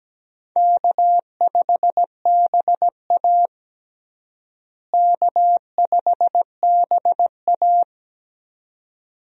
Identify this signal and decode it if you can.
Morse code, heard as a keyed tone, about 17 wpm, 700 Hz. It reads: K5BA K5BA
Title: USA Call Signs - 4 Characters - Repeated 2x - 17wpm